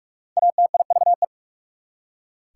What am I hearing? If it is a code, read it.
Morse code, heard as a keyed tone, about 44 wpm, 690 Hz. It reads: ATIVE
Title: Prefixes and Suffixes 45wpm